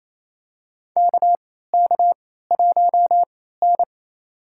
Morse code, heard as a keyed tone, about 28 wpm, spaced as for 16 wpm, 700 Hz. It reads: KK1N
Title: USA Call Signs - 3x Character Spacing - 28wpm